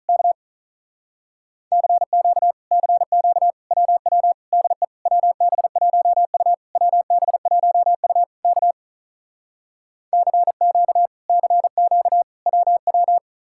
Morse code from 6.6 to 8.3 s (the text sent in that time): WB1U